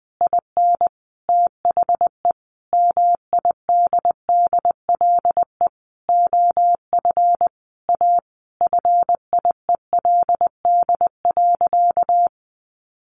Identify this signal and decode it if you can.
Morse code, heard as a keyed tone, about 20 wpm, 700 Hz. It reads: IN THE MIDDLE OF A FIELD.
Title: Sentences from Top 1000 Words 20wpm